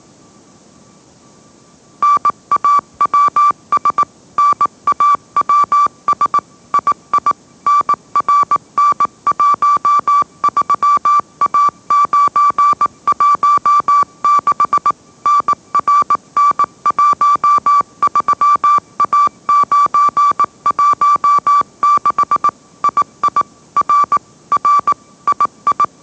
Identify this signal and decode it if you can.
Morse code, heard as a keyed tone, about 18 wpm, 1160 Hz. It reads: NAWS NAWS II NRN13A916 NRN13A916 II R R II
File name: mrx / mrx081 20wpm